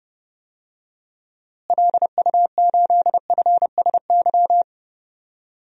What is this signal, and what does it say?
Morse code, heard as a keyed tone, about 30 wpm, 700 Hz. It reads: LU8FSY
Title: DXCC-100 Call Signs - 6 Characters - Repeated 2x - 30wpm